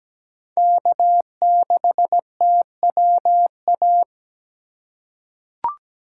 Morse code, heard as a keyed tone, about 17 wpm, 700 Hz. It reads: K6TWA
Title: USA Call Signs - 5 Characters - 8x Character Spacing - 17wpm